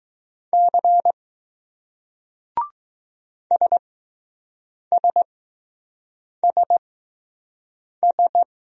Morse code, heard as a keyed tone, about 20 wpm, spaced as for 6 wpm, 700 Hz. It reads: C SSSS